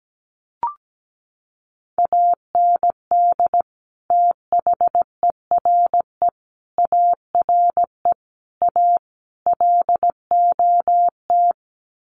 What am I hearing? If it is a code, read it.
Morse code, heard as a keyed tone, about 17 wpm, 700 Hz. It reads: AND THERE ARE A LOT
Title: Sets of 5 Words Encore2 17wpm